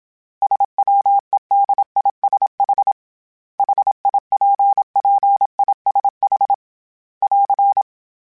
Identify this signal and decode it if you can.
Morse code, heard as a keyed tone, about 27 wpm, 800 Hz. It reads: SWEDISH HIPPISH <AR>